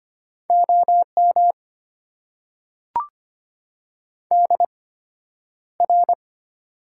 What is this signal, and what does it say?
Morse code, heard as a keyed tone, about 25 wpm, spaced as for 11 wpm, 700 Hz. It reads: OM DR
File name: Top 90 QSO Elements - 8x Character Spacing - 25wpm